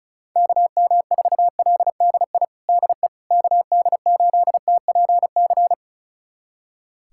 Morse code, heard as a keyed tone, about 36 wpm, 700 Hz. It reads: KM4LDI DE KD8TPC
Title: USA Callsign DE Callsign - 6 Characters - 35wpm